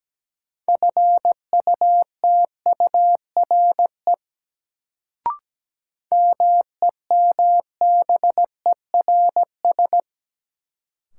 Morse code, heard as a keyed tone, about 17 wpm, 700 Hz. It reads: FUTURE MEMBERS
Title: Sets of 3 Words - New Word Drill - 17wpm